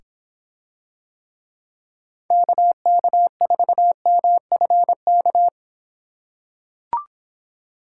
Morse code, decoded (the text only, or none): KK4MFK